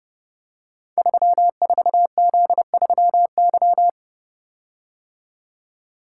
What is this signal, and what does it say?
Morse code, heard as a keyed tone, about 30 wpm, 700 Hz. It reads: 34Z3Y
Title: Code Groups - 5 Letter:Numbers - 30wpm